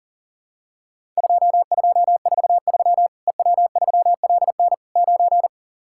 Morse code, heard as a keyed tone, about 40 wpm, 700 Hz. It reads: 2243 EW3LN 9
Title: Worked All Europe DX Contest - QTC Exchange - 40wpm